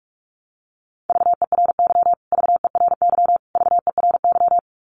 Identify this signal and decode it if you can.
Morse code, heard as a keyed tone, about 44 wpm, 700 Hz. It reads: VERY VERY VERY
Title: Top 500 Words Repeated 45wpm